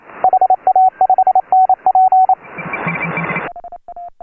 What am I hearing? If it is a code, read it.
Morse code, heard as a keyed tone, about 27 wpm, 750 Hz. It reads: HA5NP
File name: AUDIO2018 / HA3FMR CW